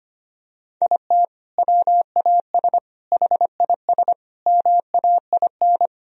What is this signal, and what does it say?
Morse code, heard as a keyed tone, about 25 wpm, 700 Hz. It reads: IT WAS HIS MAIN
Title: Sentences from Top 1000 Words - Review Part 2 - 25wpm